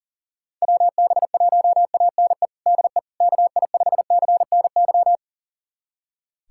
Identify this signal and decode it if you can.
Morse code, heard as a keyed tone, about 40 wpm, 700 Hz. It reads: WB1ANE DE KI5CNY
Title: USA Callsign DE Callsign - 6 Characters - 40wpm